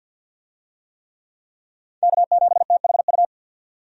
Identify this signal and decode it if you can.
Morse code, heard as a keyed tone, about 50 wpm, 700 Hz. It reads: K7THU
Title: USA Call Signs - 5 Characters - 50wpm